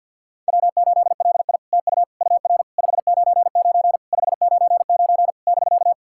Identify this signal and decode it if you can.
Morse code, heard as a keyed tone, about 50 wpm, 700 Hz. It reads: W8LI TU UR 599 599 <BK>
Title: SOTA QSO Practice 50wpm